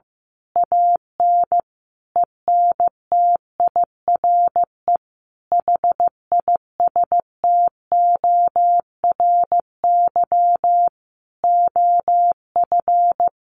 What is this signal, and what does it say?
Morse code, heard as a keyed tone, about 15 wpm, 700 Hz. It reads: AN ENTIRE HISTORY OF
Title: A Sound Of Thunder 15wpm